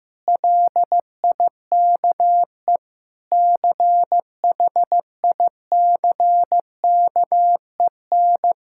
Morse code, heard as a keyed tone, about 16 wpm, 700 Hz. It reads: LIKE CHICKEN
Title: Sentences from Top 1000 Words - Review Part 6 - 15wpm